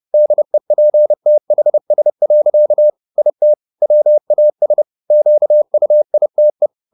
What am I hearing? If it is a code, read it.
Morse code, heard as a keyed tone, about 30 wpm, 600 Hz. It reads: DEPTHS. IT WAS QUITE
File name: the old man and the sea / the old man and the sea 1 30wpm